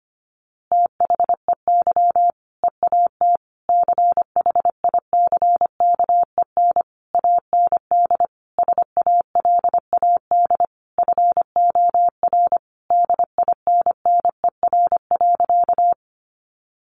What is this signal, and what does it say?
Morse code, heard as a keyed tone, about 25 wpm, 700 Hz. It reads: THEY EAT CHICKEN AND SALAD FOR DINNER.
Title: Easy Phrases 25wpm